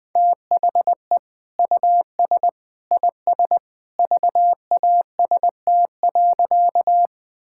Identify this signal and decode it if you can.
Morse code, heard as a keyed tone, about 20 wpm, 700 Hz. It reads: THE US IS VAST.
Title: Sentences with 2-Word Phrases - 20wpm